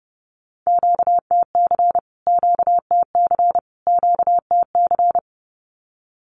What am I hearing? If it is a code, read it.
Morse code, heard as a keyed tone, about 30 wpm, 700 Hz. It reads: QTC QTC QTC
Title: Q-Codes Repeated 3x 30wpm